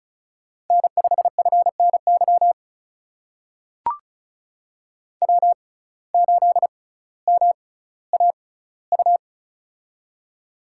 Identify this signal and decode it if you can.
Morse code, heard as a keyed tone, about 36 wpm, spaced as for 18 wpm, 700 Hz. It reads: N5FNY W8MAU
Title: USA Call Signs - 5 Characters - 6x Character Spacing - 35wpm